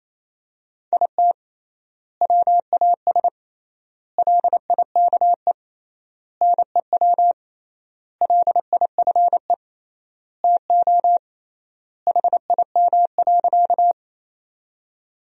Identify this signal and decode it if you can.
Morse code, heard as a keyed tone, about 28 wpm, 700 Hz. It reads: IT WAS LIKE NEW LIFE TO HIM.
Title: 3x Word-Spacing - Sentences from Top 300 Words - 28wpm